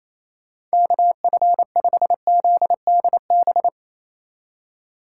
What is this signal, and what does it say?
Morse code, heard as a keyed tone, about 33 wpm, 700 Hz. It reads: KF5ZDB
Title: Speed Racing - USA 2x3 Call Signs 28wpm